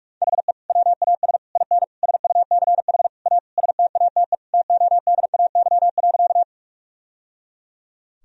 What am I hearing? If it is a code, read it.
Morse code, heard as a keyed tone, about 43 wpm, 700 Hz. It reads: HE WAS IN SUCH A STATE TODAY.
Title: Sentences from Top 1000 Words - Review Part 2 - 45wpm